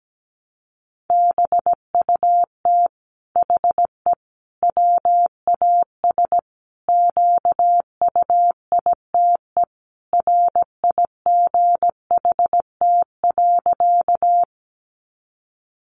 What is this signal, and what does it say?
Morse code, heard as a keyed tone, about 17 wpm, 700 Hz. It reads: BUT HE WAS QUITE RIGHT.
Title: Sentences from Top 700 Words 17wpm